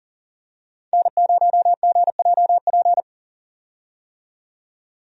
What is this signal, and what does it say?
Morse code, heard as a keyed tone, about 40 wpm, 700 Hz. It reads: N0GJP
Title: USA Call Signs - 5 Characters - 40wpm